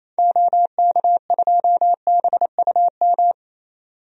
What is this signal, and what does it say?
Morse code, heard as a keyed tone, about 28 wpm, 700 Hz. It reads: OK2BUM
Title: DXCC-100 Call Signs - 6 Characters - 28wpm